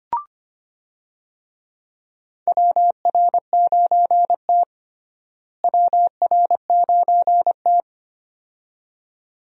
Morse code, heard as a keyed tone, about 25 wpm, 700 Hz. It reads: WR9T WR9T
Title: USA Call Signs - 4 Characters - Repeated 2x - 25wpm